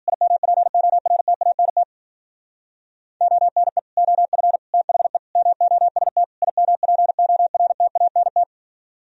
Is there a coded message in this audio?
Yes: Morse code from 0.1 to 8.4 s, heard as a keyed tone, about 52 wpm, 700 Hz. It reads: IMPORTANT ONE OF THE MOST IMPORTANT